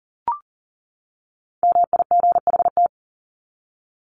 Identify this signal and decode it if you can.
Morse code, heard as a keyed tone, about 40 wpm, 700 Hz. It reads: MIGHT
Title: Top 500 Words 40wpm